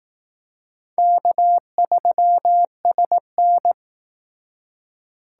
Morse code, heard as a keyed tone, about 18 wpm, 700 Hz. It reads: K3SN
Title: Speed Racing - USA Call Signs 15wpm